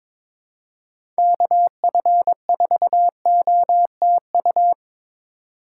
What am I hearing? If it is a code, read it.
Morse code, heard as a keyed tone, about 21 wpm, 700 Hz. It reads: KF4OTU
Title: USA Call Signs - 6 Characters - 4x Character Spacing - 22wpm